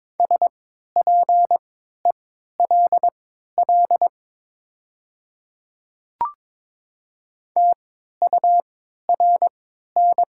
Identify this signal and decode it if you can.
Morse code, heard as a keyed tone, about 22 wpm, spaced as for 12 wpm, 700 Hz. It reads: SPELL TURN
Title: 3x Character Spacing - Top 200 Words - 22wpm